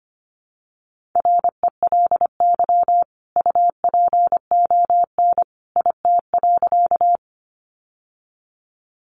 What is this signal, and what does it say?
Morse code, heard as a keyed tone, about 25 wpm, 700 Hz. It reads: RELY UPON IT.
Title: Sentences from Top 1200 Words 25wpm